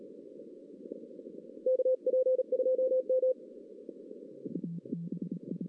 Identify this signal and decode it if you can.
Morse code, heard as a keyed tone, about 33 wpm, 520 Hz. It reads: KP2M
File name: 2016 arrl dx cw / 2016-ARRL-DX-CW-K5TR- - 0-3